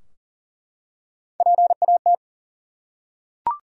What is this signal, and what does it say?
Morse code, heard as a keyed tone, about 40 wpm, 700 Hz. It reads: PAT
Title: USA Common Names - Max 3 Letters - 40wpm